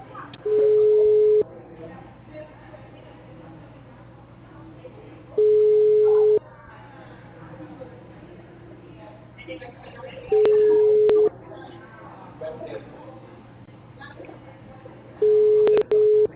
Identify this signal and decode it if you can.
Ringback tone (Europe, 425 Hz), with speech.